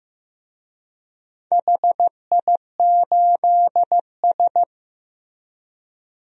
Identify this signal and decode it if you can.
Morse code, heard as a keyed tone, about 15 wpm, 700 Hz. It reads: HI8S
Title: DXCC-100 Call Signs - 4 Characters - Repeated 3x - 15wpm